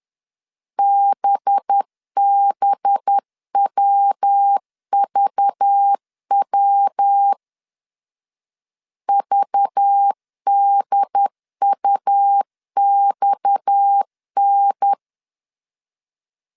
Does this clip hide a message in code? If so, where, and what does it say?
Yes: Morse code from 0.8 to 14.9 s, heard as a keyed tone, about 10 wpm, 800 Hz. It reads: BBWVW VDUXN